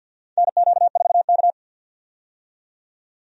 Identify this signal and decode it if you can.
Morse code, heard as a keyed tone, about 53 wpm, 700 Hz. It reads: NQ4K
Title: USA Call Signs - 4 Characters - 50wpm